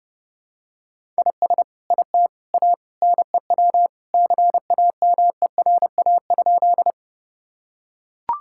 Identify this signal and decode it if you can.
Morse code, heard as a keyed tone, about 30 wpm, 700 Hz. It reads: IS IT A NEW CAMERA?
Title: Sentences from Top 1000 Words - Review Part 5 - 30wpm